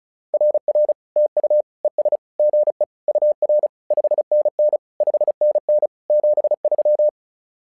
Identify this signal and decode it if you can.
Morse code, heard as a keyed tone, about 36 wpm, 600 Hz. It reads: RR TU ES GE UR 5NN 5NN 73
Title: SOTA QSO Practice 35wpm